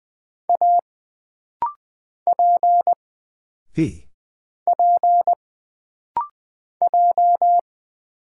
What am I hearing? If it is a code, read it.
Morse code, heard as a keyed tone, about 20 wpm, 700 Hz. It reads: A P P J
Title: Single Character (Letter:Number:Punctuation:Prosign) - Rapid-Fire - 20wpm